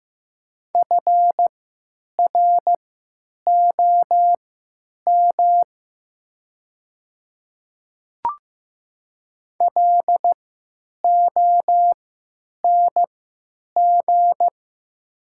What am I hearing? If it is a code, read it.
Morse code, heard as a keyed tone, about 15 wpm, spaced as for 8 wpm, 700 Hz. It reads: FROM LONG
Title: 3x Character Spacing - Top 200 Words - 15wpm